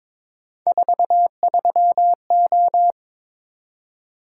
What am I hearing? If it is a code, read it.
Morse code, heard as a keyed tone, about 22 wpm, 700 Hz. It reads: 43O
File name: practice / 22wpm / Three Letter-Number Combinations - 22wpm